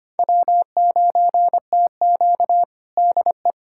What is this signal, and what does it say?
Morse code, heard as a keyed tone, about 25 wpm, 700 Hz. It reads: W9TQ DE
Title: USA Callsign DE Callsign - 4 Characters - 25wpm